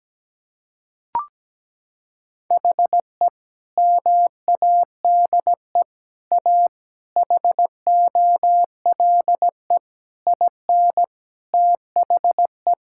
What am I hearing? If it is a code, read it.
Morse code, heard as a keyed tone, about 17 wpm, 700 Hz. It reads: HE MADE A HOLE IN THE
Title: Sentences from Top 1600 Words 17wpm